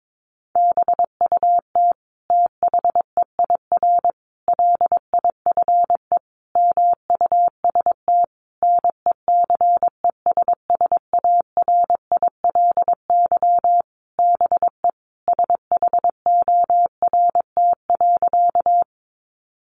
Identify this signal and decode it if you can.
Morse code, heard as a keyed tone, about 22 wpm, 700 Hz. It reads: BUT THEIR LIFE MUST NECESSARILY BE SHORT.
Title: Sentences from Top 1400 Words 22wpm